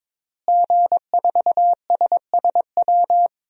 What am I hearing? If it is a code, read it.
Morse code, heard as a keyed tone, about 22 wpm, 700 Hz. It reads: G4SSW
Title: DXCC-100 Call Signs - 5 Characters - Repeated 3x - 22wpm